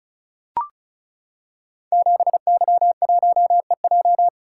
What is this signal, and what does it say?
Morse code, heard as a keyed tone, about 36 wpm, 700 Hz. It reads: 7Y1EJ
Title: Code Groups - 5 Letter:Numbers - 35wpm